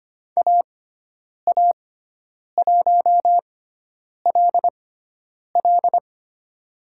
Morse code, heard as a keyed tone, about 25 wpm, spaced as for 9 wpm, 700 Hz. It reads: AA1LL
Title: USA Call Signs - 5 Characters - 6x Character Spacing - 25wpm